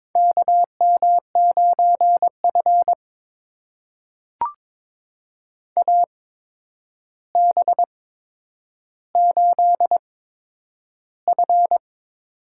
Morse code, heard as a keyed tone, about 22 wpm, spaced as for 9 wpm, 700 Hz. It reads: KM9F AB8F
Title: USA Call Signs - 4 Characters - 8x Character Spacing - 22wpm